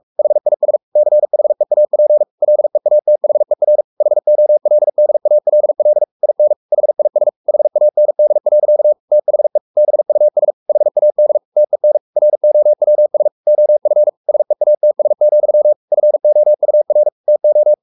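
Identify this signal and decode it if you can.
Morse code, heard as a keyed tone, about 44 wpm, 600 Hz. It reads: HIS CHEAP LEATHER HOLDALL IN HIS HAND. THE BUS HAD TEN ROWS OF SEATS, FOUR TO
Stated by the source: Morse code at 43 wpm sending Ken Follett's novel Never in English, from CW med Gnister.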